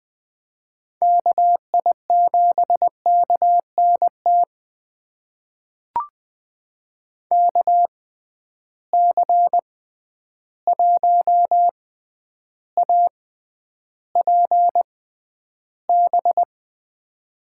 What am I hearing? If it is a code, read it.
Morse code, heard as a keyed tone, about 20 wpm, spaced as for 10 wpm, 700 Hz. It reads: KI7KNT KC1APB
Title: USA Call Signs - 6 Characters - 6x Character Spacing - 20wpm